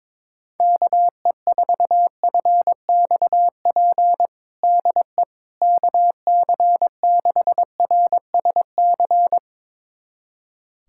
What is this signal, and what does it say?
Morse code, heard as a keyed tone, about 22 wpm, 700 Hz. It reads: KE4FXP DE KC6RSC
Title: USA Callsign DE Callsign - 6 Characters - 22wpm